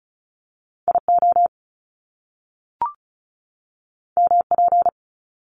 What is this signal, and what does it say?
Morse code, heard as a keyed tone, about 36 wpm, 700 Hz. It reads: IO MP
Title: Top Two Letter Combinations in Words 35wpm